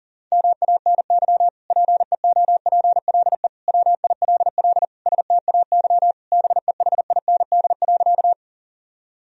Morse code, heard as a keyed tone, about 40 wpm, 700 Hz. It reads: MANY PEOPLE WILL STAY BEHIND.
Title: Sentences from Top 1000 Words - Review Part 1 - 40wpm